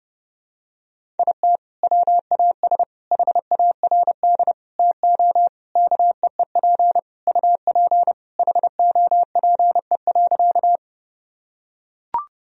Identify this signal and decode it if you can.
Morse code, heard as a keyed tone, about 30 wpm, 700 Hz. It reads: IT WAS HARD TO KEEP UP HOPE.
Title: Sentences from Top 1000 Words - Review Part 1 - 30wpm